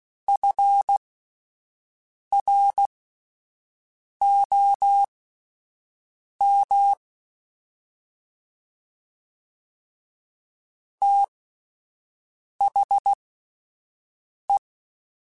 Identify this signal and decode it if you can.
Morse code, heard as a keyed tone, about 16 wpm, spaced as for 5 wpm, 800 Hz. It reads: FROM THE
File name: Spaced Characters / Spaced Characters-mp3s / Weather4 21 sp @10